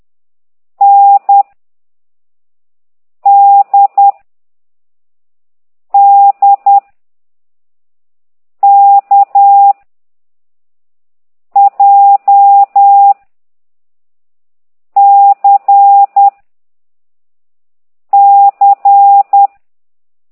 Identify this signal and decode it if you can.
Morse code, heard as a keyed tone, about 10 wpm, 800 Hz. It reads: N D D K J C C
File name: ct2idw / exerc9